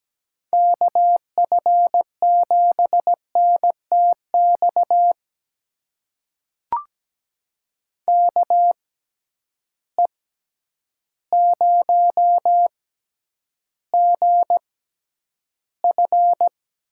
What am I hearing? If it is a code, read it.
Morse code, heard as a keyed tone, about 17 wpm, spaced as for 9 wpm, 700 Hz. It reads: KF7NTX KE0GF
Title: USA Call Signs - 6 Characters - 6x Character Spacing - 17wpm